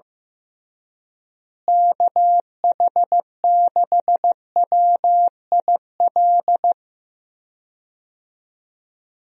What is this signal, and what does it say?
Morse code, heard as a keyed tone, about 15 wpm, 700 Hz. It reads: KH6WIL
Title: USA Call Signs - 6 Characters - Repeated 3x - 15wpm